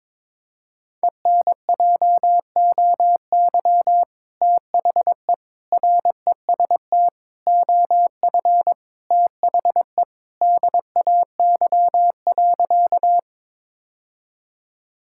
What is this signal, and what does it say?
Morse code, heard as a keyed tone, about 22 wpm, 700 Hz. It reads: ENJOY THE REST OF THE DAY.
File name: practice / 22wpm / Sentences with 2-Word Phrases - 22wpm